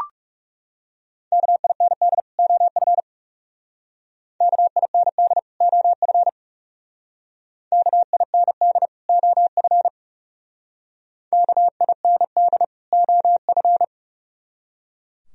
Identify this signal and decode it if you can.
Morse code, heard as a keyed tone, about 36 wpm, 700 Hz. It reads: KIND OF KIND OF KIND OF KIND OF
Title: Speed-Racing - Sets of 2 Words 30wpm